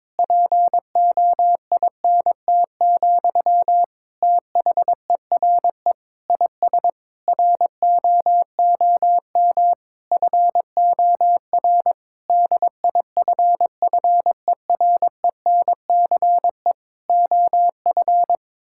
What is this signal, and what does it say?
Morse code, heard as a keyed tone, about 22 wpm, 700 Hz. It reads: POINT, THERE IS ROOM FOR DIFFERENCE OF
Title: Sentences from Top 1600 Words 22wpm